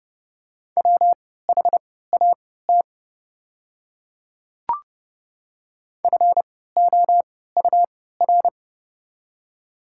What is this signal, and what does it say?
Morse code, heard as a keyed tone, about 30 wpm, spaced as for 17 wpm, 700 Hz. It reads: WHAT FOUR
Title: 3x Character Spacing - Top 300 Words - 30wpm